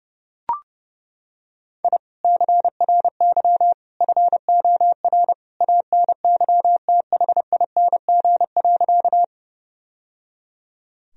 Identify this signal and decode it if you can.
Morse code, heard as a keyed tone, about 30 wpm, 700 Hz. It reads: I CRY FOR ANYTHING.
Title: Sentences from Top 1000 Words - Review Part 2 - 30wpm